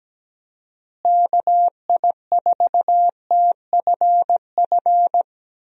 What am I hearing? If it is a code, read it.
Morse code, heard as a keyed tone, about 17 wpm, 700 Hz. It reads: KI4TFF
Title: USA Call Signs - 6 Characters - Repeated 3x - 17wpm